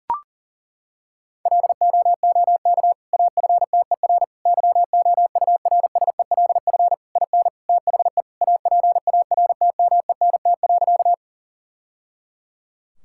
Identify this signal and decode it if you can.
Morse code, heard as a keyed tone, about 40 wpm, 700 Hz. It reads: LOOK AFTER YOURSELF IN THE APARTMENT.
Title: Sentences from Top 1000 Words - Review Part 6 - 40wpm